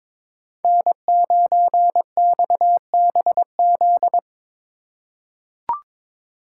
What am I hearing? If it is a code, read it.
Morse code, heard as a keyed tone, about 22 wpm, 700 Hz. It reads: N9XBZ
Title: USA Call Signs - 5 Characters - 4x Character Spacing - 22wpm